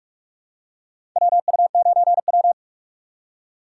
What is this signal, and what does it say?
Morse code, heard as a keyed tone, about 46 wpm, 700 Hz. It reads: WU9W
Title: USA Call Signs - 6x Character Spacing - 45wpm